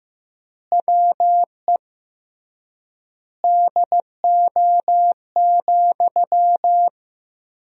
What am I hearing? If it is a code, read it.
Morse code, heard as a keyed tone, about 16 wpm, 700 Hz. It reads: WE DO,
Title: 3x Word-Spacing - Sentences from Top 100 Words - 15wpm